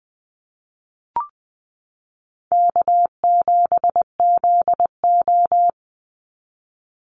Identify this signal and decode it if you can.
Morse code, heard as a keyed tone, about 20 wpm, 690 Hz. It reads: K7ZO
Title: NA SOTA Chaser Callsigns 20wpm